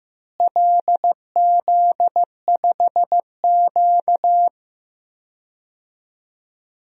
Morse code, heard as a keyed tone, about 15 wpm, 700 Hz. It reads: LZ5Q
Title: DXCC-100 Call Signs - 4 Characters - Repeated 2x - 15wpm